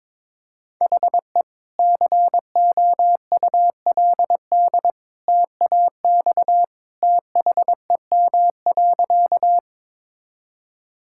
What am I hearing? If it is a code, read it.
Morse code, heard as a keyed tone, about 22 wpm, 700 Hz. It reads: HE COULD TAX THEM.
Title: Sentences from Top 1000 Words - Review Part 5 - 22wpm